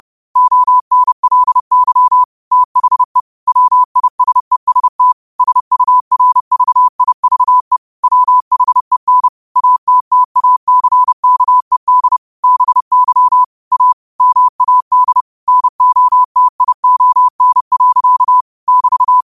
Morse code, heard as a keyed tone, about 30 wpm, 1000 Hz. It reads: ONLY THE WISEST SURVIVE WHEN ATTACKED BY A MAD NOTION. <BT>
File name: qotdfiles / 30wpm